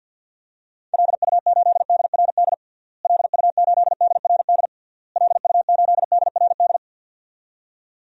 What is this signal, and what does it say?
Morse code, heard as a keyed tone, about 49 wpm, 700 Hz. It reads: LU8DRD LU8DRD LU8DRD
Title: DXCC-100 Call Signs - 6 Characters - Repeated 3x - 50wpm